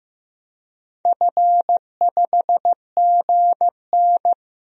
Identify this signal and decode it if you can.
Morse code, heard as a keyed tone, about 15 wpm, 700 Hz. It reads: F5GN
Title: DXCC-100 Call Signs - 4 Characters - 15wpm